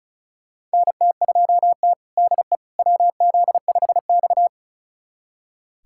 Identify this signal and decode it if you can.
Morse code, heard as a keyed tone, about 36 wpm, 700 Hz. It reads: NT2T DE WZ5X
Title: USA Callsign DE Callsign - 4 Characters - 35wpm